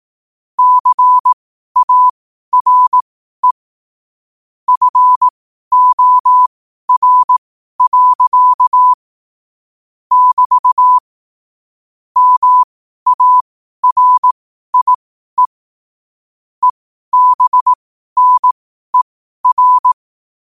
Morse code, heard as a keyed tone, about 18 wpm, spaced as for 12 wpm, 1000 Hz. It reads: CARE FOR. <BT> MARIE EBNER
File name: qotdfiles / 12wpm